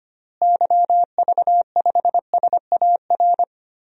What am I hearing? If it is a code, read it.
Morse code, heard as a keyed tone, about 25 wpm, 700 Hz. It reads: YV5SAR
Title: DXCC-100 Call Signs - 6 Characters - Repeated 3x - 25wpm